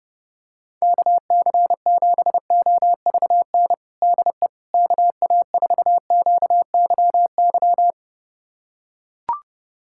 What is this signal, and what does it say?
Morse code, heard as a keyed tone, about 30 wpm, 700 Hz. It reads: KC7OVN DE KA4QYY
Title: USA Callsign DE Callsign 30wpm